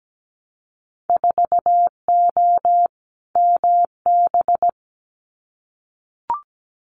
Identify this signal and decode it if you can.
Morse code, heard as a keyed tone, about 17 wpm, 700 Hz. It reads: 4O MB
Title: Winter Field Day Exchanges Encore 17wpm